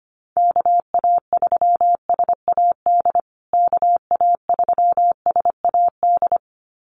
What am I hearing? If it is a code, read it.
Morse code, heard as a keyed tone, about 25 wpm, 700 Hz. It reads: KA3SAD KA3SAD
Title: USA Call Signs Repeated 2x 25wpm